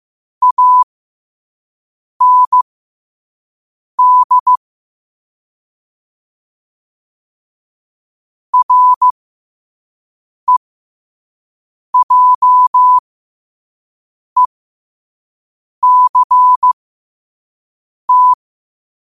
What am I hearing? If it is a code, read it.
Morse code, heard as a keyed tone, about 15 wpm, spaced as for 5 wpm, 1000 Hz. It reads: AND REJECT
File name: qotdfiles / 5wpm